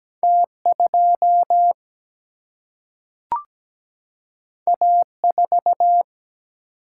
Morse code, heard as a keyed tone, about 17 wpm, 700 Hz. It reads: T 2 A 4
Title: Single-Letter Single-Number 17wpm